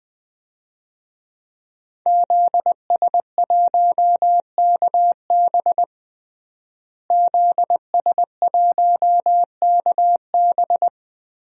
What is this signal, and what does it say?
Morse code, heard as a keyed tone, about 20 wpm, 700 Hz. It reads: ZS1KB ZS1KB
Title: DXCC-100 Call Signs - 5 Characters - Repeated 2x - 20wpm